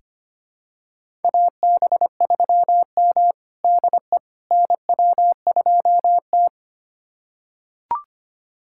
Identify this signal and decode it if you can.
Morse code, heard as a keyed tone, about 25 wpm, 700 Hz. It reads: AB3M DE NW2T
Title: USA Callsign DE Callsign - 4 Characters - 25wpm